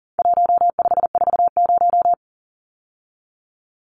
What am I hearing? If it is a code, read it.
Morse code, heard as a keyed tone, about 40 wpm, 700 Hz. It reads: 1540
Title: Numbers - 4-Digit 40wpm